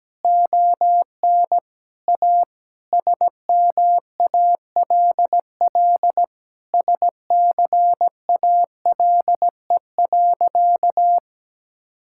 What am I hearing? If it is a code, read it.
Morse code, heard as a keyed tone, about 17 wpm, 700 Hz. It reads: ON A SMALL SCALE.
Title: Sentences from Top 1000 Words - Review Part 5 - 17wpm